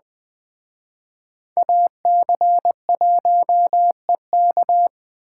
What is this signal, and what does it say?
Morse code, heard as a keyed tone, about 21 wpm, 700 Hz. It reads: AC1EK
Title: Speed Racing - USA Call Signs 17wpm